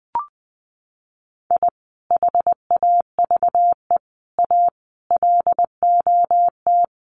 Morse code, heard as a keyed tone, about 19 wpm, 700 Hz. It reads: I HAVE A LOT
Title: Sets of 5 Words 20wpm